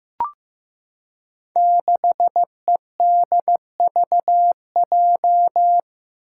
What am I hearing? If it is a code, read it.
Morse code, heard as a keyed tone, about 15 wpm, 700 Hz. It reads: 6EDVJ
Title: Code Groups - 5 Letter:Numbers - 15wpm